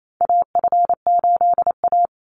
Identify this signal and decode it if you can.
Morse code, heard as a keyed tone, about 28 wpm, 700 Hz. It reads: AF8A
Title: Call Signs - 28wpm